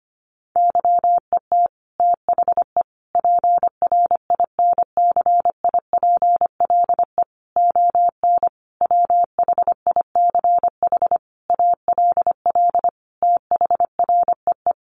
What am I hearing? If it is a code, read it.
Morse code, heard as a keyed tone, about 25 wpm, 700 Hz. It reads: YET THE PRINCIPLE ON WHICH ALL THREE
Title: Sentences from Top 1400 Words 25wpm